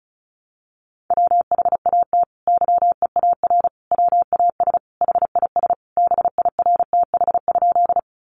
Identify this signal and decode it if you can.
Morse code, heard as a keyed tone, about 34 wpm, 700 Hz. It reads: WHAT YEAR WAS HIS BIRTH?
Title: Sentences from Top 800 Words 35wpm